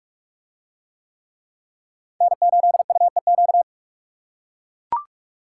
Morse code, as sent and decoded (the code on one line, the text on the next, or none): -. ---.. ..- . --.-
N8UEQ